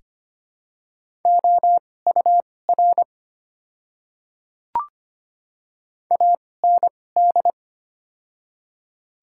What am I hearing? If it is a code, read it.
Morse code, heard as a keyed tone, about 25 wpm, spaced as for 18 wpm, 700 Hz. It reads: OUR AND
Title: 2x Character Spacing - Top 300 Words - 25wpm